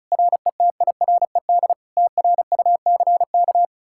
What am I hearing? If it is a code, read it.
Morse code, heard as a keyed tone, about 47 wpm, spaced as for 42 wpm, 700 Hz. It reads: RETIRED TRUCK
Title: Speed-Racing Common QSO Phrases - 35wpm